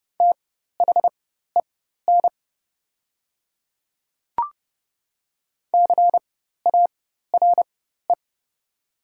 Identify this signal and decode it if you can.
Morse code, heard as a keyed tone, about 30 wpm, spaced as for 14 wpm, 690 Hz. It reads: THEN CARE
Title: 4x Character Spacing - Top 300 Words - 30wpm